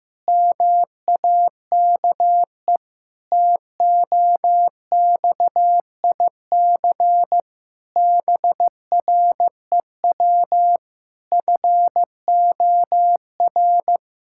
Morse code, heard as a keyed tone, about 15 wpm, 700 Hz. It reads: MAKE TOXIC BREW FOR
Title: Pangram Challenge v2 - 15wpm